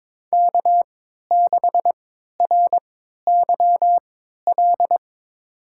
Morse code, heard as a keyed tone, about 22 wpm, spaced as for 13 wpm, 700 Hz. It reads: K6RYL
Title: USA Call Signs - 5 Characters - 3x Character Spacing - 22wpm